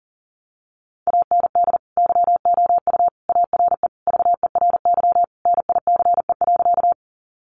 Morse code, heard as a keyed tone, about 40 wpm, 700 Hz. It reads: AND YOU ARE VERY NICE.
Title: Sentences from Top 1100 Words 40wpm